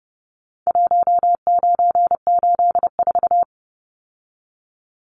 Morse code, heard as a keyed tone, about 30 wpm, 700 Hz. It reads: 1984
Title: Numbers - Contest Style - 4-Digit 30wpm